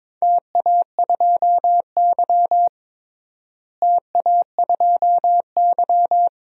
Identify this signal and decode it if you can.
Morse code, heard as a keyed tone, about 22 wpm, 700 Hz. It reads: TA2Y TA2Y
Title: DXCC-100 Call Signs - 4 Characters - Repeated 2x - 22wpm